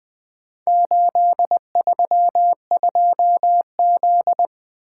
Morse code, heard as a keyed tone, about 24 wpm, 700 Hz. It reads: 832Z
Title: Speed-Racing - Four Letter-Number Combinations - 20wpm